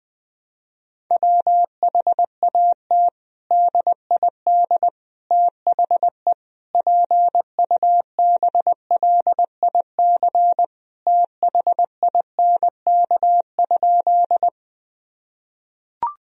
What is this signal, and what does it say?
Morse code, heard as a keyed tone, about 20 wpm, 700 Hz. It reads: WHAT DID THE PUBLIC THINK?
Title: Sentences from Top 1000 Words - Review Part 3 - 20wpm